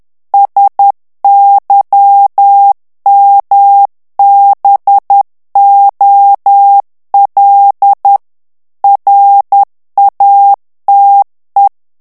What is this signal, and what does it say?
Morse code, heard as a keyed tone, about 10 wpm, 800 Hz. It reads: SYMBOL RATE
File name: ANGLAIS texte clair 3 vitesse 10 mots minute